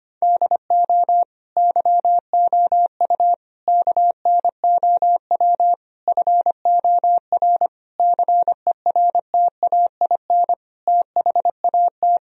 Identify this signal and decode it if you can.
Morse code, heard as a keyed tone, about 25 wpm, 700 Hz. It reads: DO YOU KNOW FOR CERTAIN THAT
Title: Sentences from Top 1000 Words - Review Part 2 - 25wpm